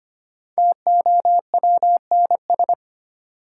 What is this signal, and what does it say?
Morse code, heard as a keyed tone, about 25 wpm, 700 Hz. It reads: TOWNS
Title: Sentences from Top 1000 Words - Review Part 6 - New Word Drill - 25wpm